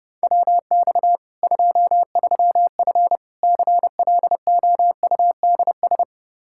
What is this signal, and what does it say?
Morse code, heard as a keyed tone, about 32 wpm, 700 Hz. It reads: WX 23F CLOUDS
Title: Practice QSO - Exchange 2 - Info - 30wpm